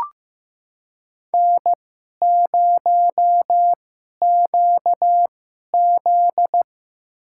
Morse code, heard as a keyed tone, about 15 wpm, spaced as for 11 wpm, 700 Hz. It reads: N0QZ
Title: USA Call Signs - 4 Characters - 2x Character Spacing - 15wpm